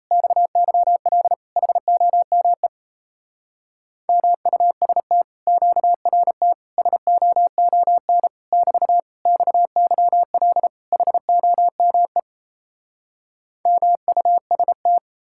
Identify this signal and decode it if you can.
Morse code, heard as a keyed tone, about 38 wpm, 700 Hz. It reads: XYL HOME MUST QRT SOON <BT> XYL HOME MUST